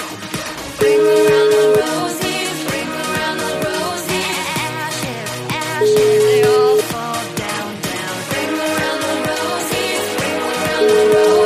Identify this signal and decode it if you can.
Ringback tone (Europe, 425 Hz), with noise.